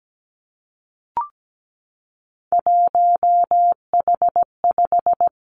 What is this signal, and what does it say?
Morse code, heard as a keyed tone, about 17 wpm, 700 Hz. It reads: 1H5
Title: Three Challenging Letter-Number Combinations 17wpm